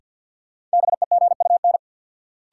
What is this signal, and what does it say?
Morse code, heard as a keyed tone, about 51 wpm, 700 Hz. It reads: BEGUN
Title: Sentences from Top 1000 Words - Review Part 6 - New Word Drill - 50wpm